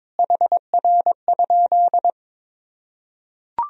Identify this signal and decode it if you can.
Morse code, heard as a keyed tone, about 22 wpm, 700 Hz. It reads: HR?
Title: Sentences from Top 500 Words and Abbreviations 22wpm